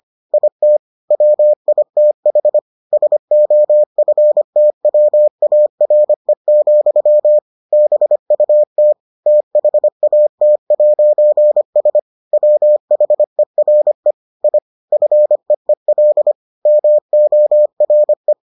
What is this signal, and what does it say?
Morse code, heard as a keyed tone, about 25 wpm, 600 Hz. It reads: IT WITH SOFTWARE, BUT THAT'S WHERE I FEEL MORE